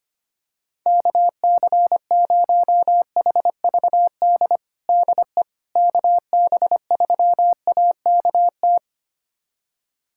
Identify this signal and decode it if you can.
Morse code, heard as a keyed tone, about 25 wpm, 700 Hz. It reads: KC0HVD DE KB3AKT
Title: USA Callsign DE Callsign 25wpm